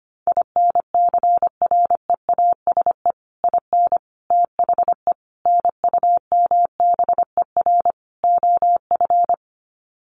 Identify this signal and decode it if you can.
Morse code, heard as a keyed tone, about 25 wpm, 700 Hz. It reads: INCREASE IN THE NUMBER OF
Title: Sets of 5 Words 25wpm